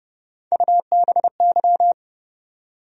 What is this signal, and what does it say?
Morse code, heard as a keyed tone, about 32 wpm, 700 Hz. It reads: UBY
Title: USA Call Sign Postfixes - 3 Letters 30wpm